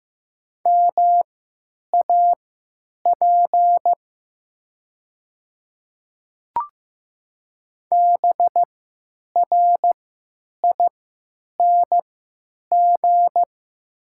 Morse code, heard as a keyed tone, about 15 wpm, spaced as for 9 wpm, 700 Hz. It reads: MAP BRING
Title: 3x Character Spacing - Top 500 Words - 15wpm